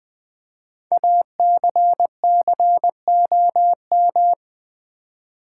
Morse code, heard as a keyed tone, about 20 wpm, 700 Hz. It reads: ACCOM
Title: Build A Word - 5 Letters At A Time - 10-Letter Words - 20wpm